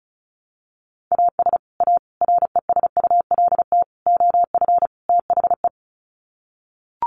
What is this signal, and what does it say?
Morse code, heard as a keyed tone, about 34 wpm, 700 Hz. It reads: AS A RESULT OF THE
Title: Sets of 5 Words 35wpm